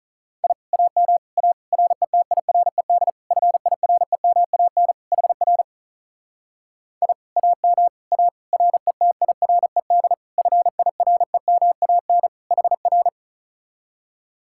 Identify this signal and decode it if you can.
Morse code, heard as a keyed tone, about 46 wpm, 700 Hz. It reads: I AM A RETIRED FIREMAN HR I AM A RETIRED FIREMAN HR